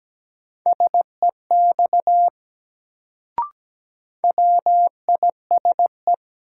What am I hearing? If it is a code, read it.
Morse code, heard as a keyed tone, about 17 wpm, 700 Hz. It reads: SEX WISE
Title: Sentences from Top 1600 Words - New Word Drill - 17wpm